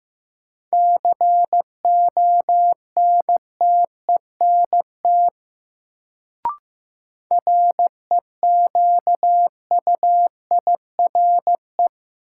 Morse code, heard as a keyed tone, about 15 wpm, 700 Hz. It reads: CONTENT REQUIRE
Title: Sentences from Top 900 Words - New Word Drill - 15wpm